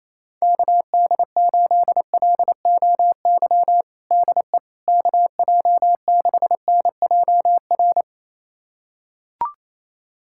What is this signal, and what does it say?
Morse code, heard as a keyed tone, about 28 wpm, 700 Hz. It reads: KD8LOY DE KJ6NJR
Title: USA Callsign DE Callsign - 6 Characters - 28wpm